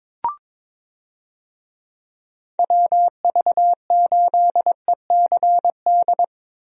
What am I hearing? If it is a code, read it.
Morse code, heard as a keyed tone, about 22 wpm, 700 Hz. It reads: WV8ECD
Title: USA Call Signs - 6 Characters - 22wpm